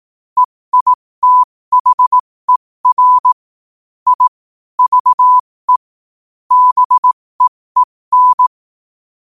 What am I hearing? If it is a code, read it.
Morse code, heard as a keyed tone, about 18 wpm, spaced as for 15 wpm, 1000 Hz. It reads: EITHER I VE BEEN
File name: qotdfiles / 15wpm